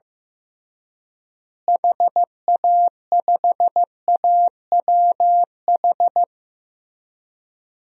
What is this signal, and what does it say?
Morse code, heard as a keyed tone, about 15 wpm, 700 Hz. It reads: HA5AWH
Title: DXCC-100 Call Signs - 6 Characters - Repeated 3x - 15wpm